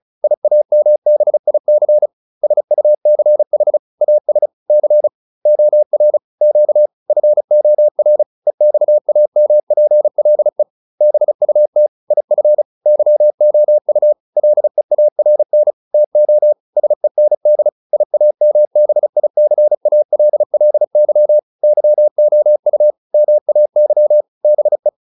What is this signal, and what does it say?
Morse code, heard as a keyed tone, about 35 wpm, 600 Hz. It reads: IAMBIC SUCH AS C OR Q FOR EXAMPLE BUT IF YOU LEARN TO SEND IAMBICALLY YOU MAY BE